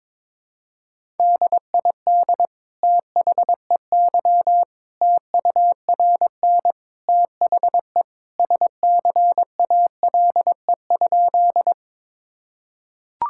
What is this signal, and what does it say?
Morse code, heard as a keyed tone, about 21 wpm, 700 Hz. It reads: DID THEY TURN THE SCALE?
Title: Sentences from Top 1000 Words - Review Part 5 - 22wpm